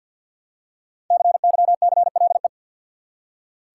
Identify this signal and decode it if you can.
Morse code, heard as a keyed tone, about 46 wpm, 700 Hz. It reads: CYCLE
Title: Five Letter Words - 50wpm